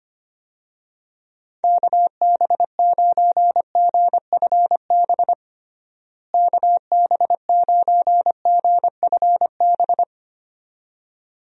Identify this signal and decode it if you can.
Morse code, heard as a keyed tone, about 25 wpm, 700 Hz. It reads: KB9GFB KB9GFB
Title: USA Call Signs - 6 Characters - Repeated 2x - 25wpm